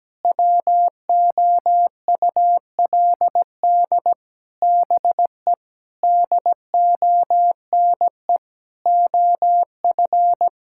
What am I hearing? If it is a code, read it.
Morse code, heard as a keyed tone, about 17 wpm, 700 Hz. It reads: WOULD BE DONE OF
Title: Sentences from Top 1000 Words - Review Part 5 - 17wpm